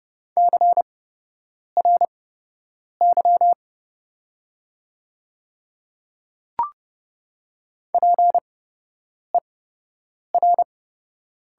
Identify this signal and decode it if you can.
Morse code, heard as a keyed tone, about 32 wpm, spaced as for 8 wpm, 700 Hz. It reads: CRY PER
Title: 8x Character Spacing - Top 500 Words - 30wpm